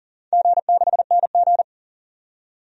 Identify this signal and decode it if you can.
Morse code, heard as a keyed tone, about 39 wpm, 700 Hz. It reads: G6NG
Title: DXCC-100 Call Signs - 4 Characters - 40wpm